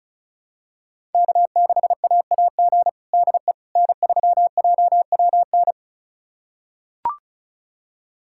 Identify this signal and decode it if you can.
Morse code, heard as a keyed tone, about 36 wpm, 700 Hz. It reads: K6AAG DE N3JWN
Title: USA Callsign DE Callsign - 5 Characters - 35wpm